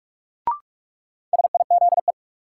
Morse code, heard as a keyed tone, about 46 wpm, 700 Hz. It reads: SIZE
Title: Sentences from Top 400 Words - New Word Drill - 45wpm